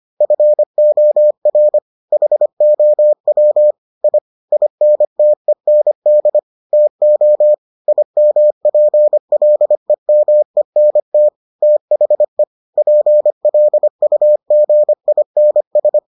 Morse code, heard as a keyed tone, about 25 wpm, 600 Hz. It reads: FOR HOW I INTEND TO IMPLEMENT THE PLUGINS